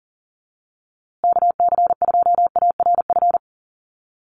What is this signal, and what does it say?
Morse code, heard as a keyed tone, about 40 wpm, 700 Hz. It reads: KC2ARF
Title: USA Call Signs Repeated 2x 40wpm